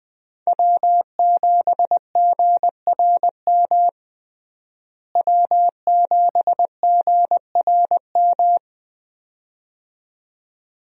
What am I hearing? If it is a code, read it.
Morse code, heard as a keyed tone, about 20 wpm, 700 Hz. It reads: W7GRM W7GRM
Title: USA Call Signs - 5 Characters - Repeated 2x - 20wpm